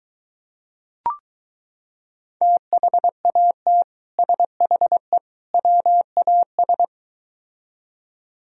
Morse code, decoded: THAT SHE WAS

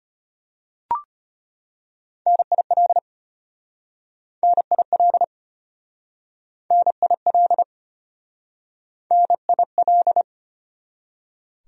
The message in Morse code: NIL NIL NIL NIL

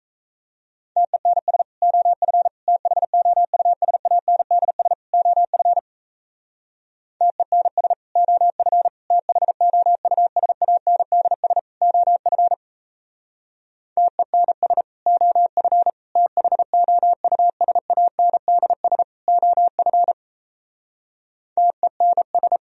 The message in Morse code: TENS OF THOUSANDS OF TENS OF THOUSANDS OF TENS OF THOUSANDS OF TENS

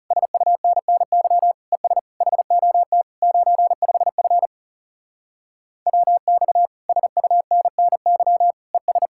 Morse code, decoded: SUNNY ES HOT 95F WX SUNNY ES

